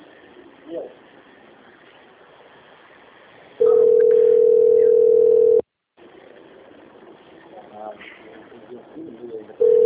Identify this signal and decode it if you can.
Ringback tone (North America), 440 + 480 Hz, with speech.
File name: out- - - - - .12216